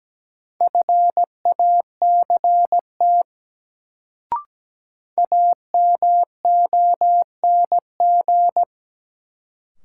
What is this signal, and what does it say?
Morse code, heard as a keyed tone, about 17 wpm, 700 Hz. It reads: FACT AMONG
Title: Sets of 2 Words - New Word Drill - 17wpm